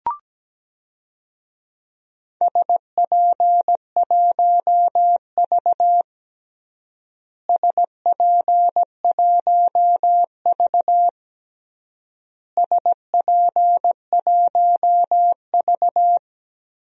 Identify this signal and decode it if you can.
Morse code, heard as a keyed tone, about 17 wpm, 700 Hz. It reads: SP1V SP1V SP1V
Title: DXCC-100 Call Signs - 4 Characters - Repeated 3x - 17wpm